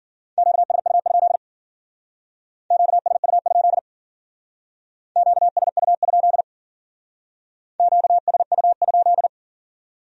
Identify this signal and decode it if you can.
Morse code, heard as a keyed tone, about 51 wpm, 700 Hz. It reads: QSU? QSU? QSU? QSU?